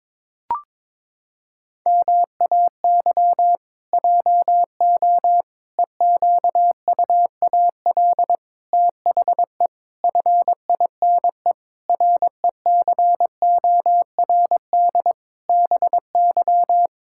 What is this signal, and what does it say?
Morse code, heard as a keyed tone, about 22 wpm, 700 Hz. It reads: MAY JO EQUAL THE FINE RECORD BY
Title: Pangram Challenge v2 - 22wpm